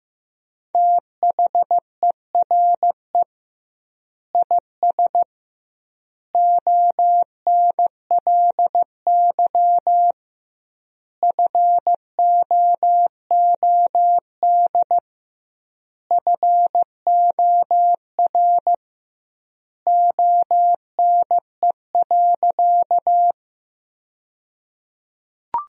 Morse code, heard as a keyed tone, about 15 wpm, 700 Hz. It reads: THERE IS ONLY FOOD FOR ONE.
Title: 2x Word-Spacing - Sentences from Top 300 Words - 15wpm